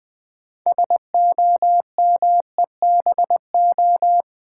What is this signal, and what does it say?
Morse code, heard as a keyed tone, about 20 wpm, 690 Hz. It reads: SOMEBO
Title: Build A Word - 2 Letters At A Time - 8-Letter Words - 20wpm